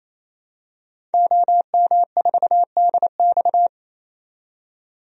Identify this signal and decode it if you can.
Morse code, heard as a keyed tone, about 28 wpm, 700 Hz. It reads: OM4DX
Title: DXCC-100 Call Signs - 5 Characters - 28wpm